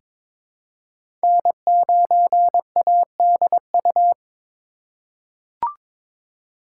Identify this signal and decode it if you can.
Morse code, heard as a keyed tone, about 21 wpm, 700 Hz. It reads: N9ADU
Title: USA Call Signs - 5 Characters - 3x Character Spacing - 22wpm